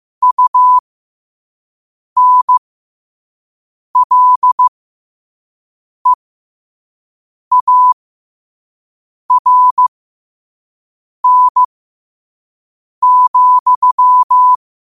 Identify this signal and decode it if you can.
Morse code, heard as a keyed tone, about 15 wpm, spaced as for 5 wpm, 1000 Hz. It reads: UNLEARN,
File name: qotdfiles / 5wpm